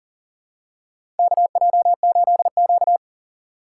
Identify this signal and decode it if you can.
Morse code, heard as a keyed tone, about 40 wpm, 700 Hz. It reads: KJ8Q
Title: USA Call Signs - 4 Characters - 3x Character Spacing - 40wpm